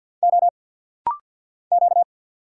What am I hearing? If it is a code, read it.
Morse code, heard as a keyed tone, about 49 wpm, 690 Hz. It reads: O Q
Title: Single Letters - Rapid-Fire 50wpm